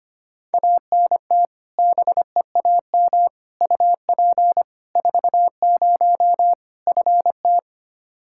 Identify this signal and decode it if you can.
Morse code, heard as a keyed tone, about 25 wpm, 700 Hz. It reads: ANT BEAM UP 40 FT
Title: Practice QSO - Exchange 2 - Info - 25wpm